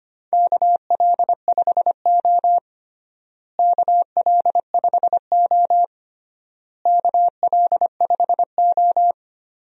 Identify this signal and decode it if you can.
Morse code, heard as a keyed tone, about 25 wpm, 700 Hz. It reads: KL5O KL5O KL5O
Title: USA Call Signs - 4 Characters - Repeated 3x - 25wpm